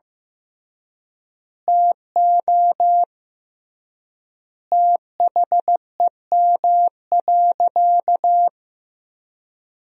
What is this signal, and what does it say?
Morse code, heard as a keyed tone, about 15 wpm, 700 Hz. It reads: TO THEM.
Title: 3x Word-Spacing - Sentences from Top 500 Words - 15wpm